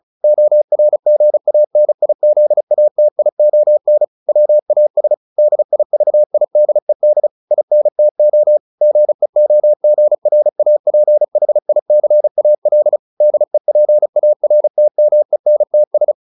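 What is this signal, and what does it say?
Morse code, heard as a keyed tone, about 34 wpm, 600 Hz. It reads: ORGANIZATION WAS DIVIDED INTO GEOGRAPHICAL DEPARTMENTS
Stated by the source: Morse practice at 35 wpm from CW med Gnister, sending Ken Follett's novel Never in English.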